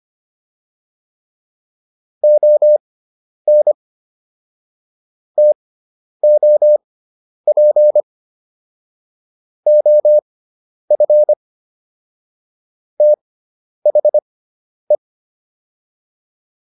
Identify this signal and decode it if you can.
Morse code, heard as a keyed tone, about 25 wpm, spaced as for 10 wpm, 600 Hz. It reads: ON TOP OF THE